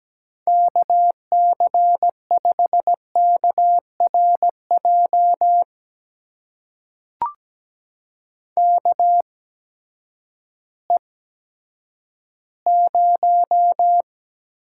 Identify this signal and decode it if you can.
Morse code, heard as a keyed tone, about 17 wpm, spaced as for 10 wpm, 700 Hz. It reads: KC5KRJ KE0
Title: USA Call Signs - 6 Characters - 8x Character Spacing - 17wpm